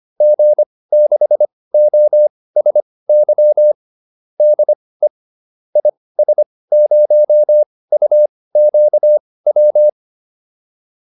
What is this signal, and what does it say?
Morse code, heard as a keyed tone, about 25 wpm, spaced as for 18 wpm, 600 Hz. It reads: G6OSY DE IS0UQW